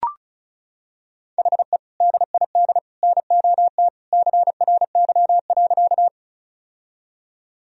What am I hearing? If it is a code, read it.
Morse code, heard as a keyed tone, about 35 wpm, 700 Hz. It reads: HE DID NOT CRY.
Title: Sentences from Top 1000 Words - Review Part 2 - 35wpm